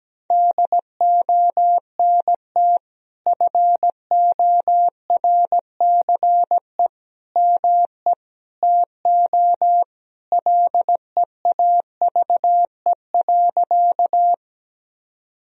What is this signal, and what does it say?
Morse code, heard as a keyed tone, about 17 wpm, 700 Hz. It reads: DONT FORCE ME TO LEAVE.
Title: Sentences from Top 1000 Words - Review Part 3 - 17wpm